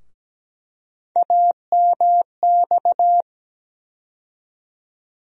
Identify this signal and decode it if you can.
Morse code, heard as a keyed tone, about 17 wpm, 700 Hz. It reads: AMX
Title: USA Call Sign Postfixes - 3 Letters 17wpm